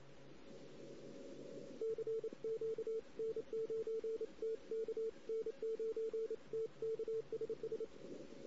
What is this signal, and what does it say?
Morse code, heard as a keyed tone, about 27 wpm, 450 Hz. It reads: CQ N9TK N9TK SS
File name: CW / 2008-ss-cw-k5tr-op-n5rz -16:00